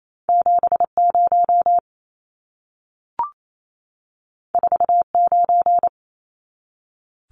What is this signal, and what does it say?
Morse code, heard as a keyed tone, about 28 wpm, 700 Hz. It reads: 70 49
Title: Numbers - 2-Digit 28wpm